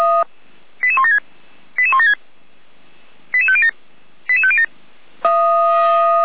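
Five-tone selective call (ZVEI), heard as quick runs of five tones, tones of about 80 ms. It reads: 88176 88176 88507 88507
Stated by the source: SDR-received ZVEI five-tone selective call, code 88391.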